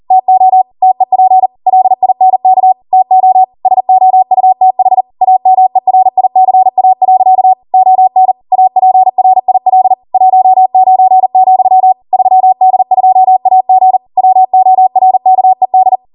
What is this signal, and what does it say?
Morse code, heard as a keyed tone, about 40 wpm, 750 Hz. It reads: TO TEP LINK TO SOUTH AMERICA. ON APRIL 19, 3D2AG WORKED